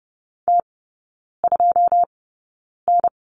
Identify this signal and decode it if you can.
Morse code, heard as a keyed tone, about 30 wpm, spaced as for 9 wpm, 700 Hz. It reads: T2N
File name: 081 Lesson 14e - Number 2 - Three Characters 30wpm